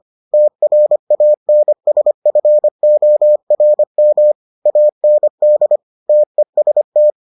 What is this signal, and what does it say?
Morse code, heard as a keyed tone, about 25 wpm, 600 Hz. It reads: TRANSFORM AND TEST